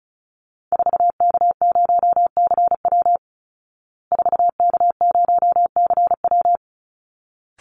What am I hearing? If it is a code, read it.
Morse code, heard as a keyed tone, about 34 wpm, 700 Hz. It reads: 4K0CW 4K0CW
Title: Call Signs Repeated 3x - 35wpm